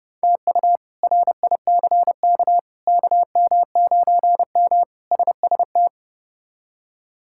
Morse code, heard as a keyed tone, about 30 wpm, 700 Hz. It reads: TU RICK KM9M SST
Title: K1USN SST QSO Practice - 30wpm